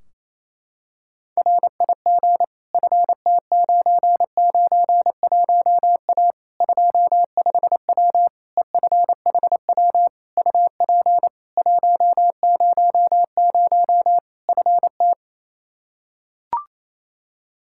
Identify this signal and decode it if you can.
Morse code, heard as a keyed tone, about 27 wpm, 700 Hz. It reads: RIG FT991A 25W EFHW UP 100 FT
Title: Practice QSO - Exchange 2 - Info - 28wpm